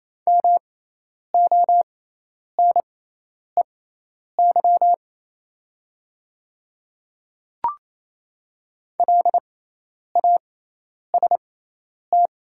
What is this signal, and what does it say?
Morse code, heard as a keyed tone, about 28 wpm, spaced as for 10 wpm, 700 Hz. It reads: MONEY LAST
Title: 6x Character Spacing - Top 500 Words - 28wpm